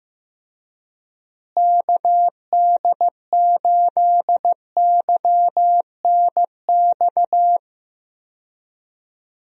Morse code, heard as a keyed tone, about 16 wpm, 700 Hz. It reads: KD8YNX
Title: USA Call Signs - 6 Characters - Repeated 2x - 15wpm